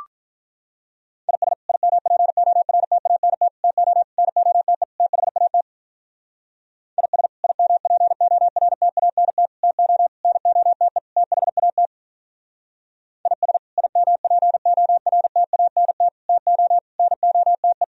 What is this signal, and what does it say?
Morse code, heard as a keyed tone, about 44 wpm, 700 Hz. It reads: IS IMPORTANT TO NOTE THAT IS IMPORTANT TO NOTE THAT IS IMPORTANT TO NOTE
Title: Speed-Racing - Sets of 5 Words 35wpm